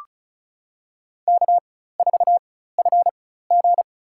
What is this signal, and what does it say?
Morse code, heard as a keyed tone, about 34 wpm, spaced as for 16 wpm, 700 Hz. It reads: K4FG
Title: USA Call Signs - 4 Characters - 4x Character Spacing - 35wpm